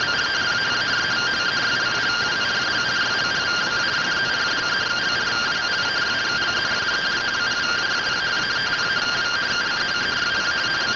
Two-tone teleprinter signal, about 170 Hz shift.
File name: findings / CT AXIAL -572mm 2.gen-37 rtty rx